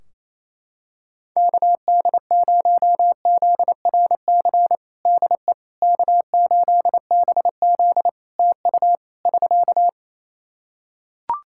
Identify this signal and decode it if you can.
Morse code, heard as a keyed tone, about 28 wpm, 700 Hz. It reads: KD0ZRC DE K8BZ TU <SK>
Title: Practice QSO - Exchange 4 - Ending - 28wpm